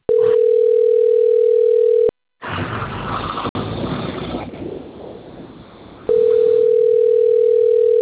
Ringback tone (North America), 440 + 480 Hz.